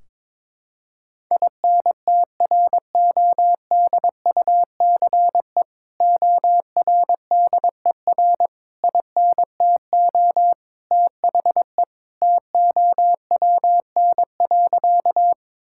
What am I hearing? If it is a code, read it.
Morse code, heard as a keyed tone, about 22 wpm, 700 Hz. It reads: INTRODUCE ORDER INTO THE TOWN.
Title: Sentences from Top 1600 Words 22wpm